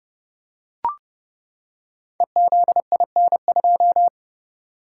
Morse code, heard as a keyed tone, about 30 wpm, 700 Hz. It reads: EZIN2
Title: Code Groups - 5 Letter:Numbers - 30wpm